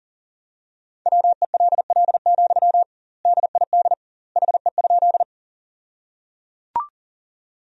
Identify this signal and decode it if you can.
Morse code, heard as a keyed tone, about 37 wpm, 700 Hz. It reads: WELL, DID HE?
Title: 2x Word-Spacing - Sentences from Top 200 Words - 40wpm